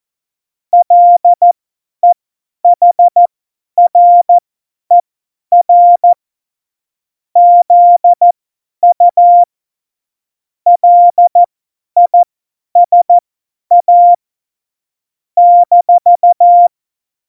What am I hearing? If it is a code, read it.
Morse code, heard as a keyed tone, about 14 wpm, spaced as for 10 wpm, 700 Hz. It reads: LEHRER ZU LISA -
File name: Morsekurs / Witze / Witz 07. Mai 2019, 50 F70